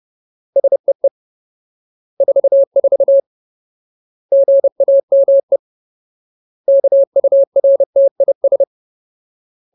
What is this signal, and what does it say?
Morse code, heard as a keyed tone, about 30 wpm, 550 Hz. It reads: SEE 44 GAME KURTIS
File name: Ultimate Word List / 30WPM Ultimate